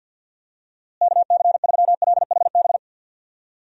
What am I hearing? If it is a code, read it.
Morse code, heard as a keyed tone, about 51 wpm, 700 Hz. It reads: KC3LHB
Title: USA Call Signs - 6 Characters - Repeated 3x - 50wpm